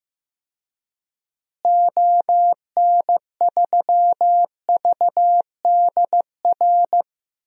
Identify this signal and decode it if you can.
Morse code, heard as a keyed tone, about 15 wpm, 700 Hz. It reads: ON3VDR
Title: DXCC-100 Call Signs - 6 Characters - Repeated 2x - 15wpm